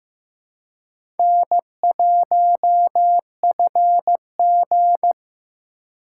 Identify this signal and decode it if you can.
Morse code, heard as a keyed tone, about 15 wpm, 700 Hz. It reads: N1FG
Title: USA Call Signs - 4 Characters - 3x Character Spacing - 15wpm